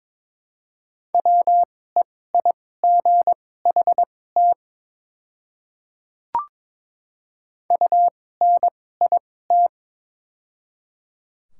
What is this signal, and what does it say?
Morse code, heard as a keyed tone, about 22 wpm, spaced as for 16 wpm, 700 Hz. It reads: WEIGHT UNIT
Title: 2x Character Spacing - Top 500 Words - 22wpm